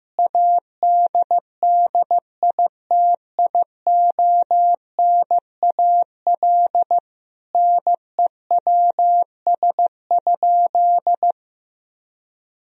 Morse code, heard as a keyed tone, about 16 wpm, 700 Hz. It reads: ADDITIONAL NEWS?
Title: Sentences from Top 1000 Words - Review Part 4 - 15wpm